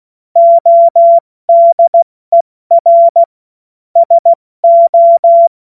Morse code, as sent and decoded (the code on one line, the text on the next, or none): --- -.. .-. / ... ---
ODER SO